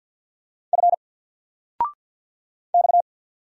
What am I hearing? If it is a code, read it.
Morse code, heard as a keyed tone, about 50 wpm, 700 Hz. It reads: F X
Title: Single Letter-Number 50wpm